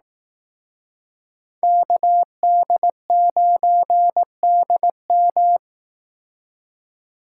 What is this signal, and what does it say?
Morse code, heard as a keyed tone, about 18 wpm, 700 Hz. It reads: KD9DM
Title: Speed Racing - USA 2x2 Call Signs 15wpm